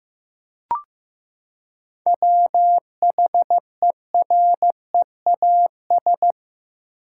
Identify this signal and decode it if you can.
Morse code, heard as a keyed tone, about 15 wpm, 700 Hz. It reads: WHEREAS
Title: Seven Letter Words - 15wpm